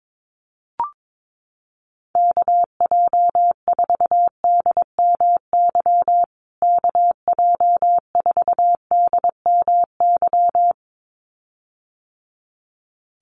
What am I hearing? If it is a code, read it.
Morse code, heard as a keyed tone, about 22 wpm, 700 Hz. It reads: KJ4DMY KJ4DMY
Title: USA Call Signs Repeated 2x 22wpm